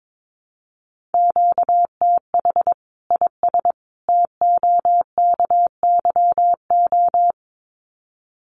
Morse code, heard as a keyed tone, about 22 wpm, 700 Hz. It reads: QTH IS TOKYO
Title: Common QSO Phrases 22wpm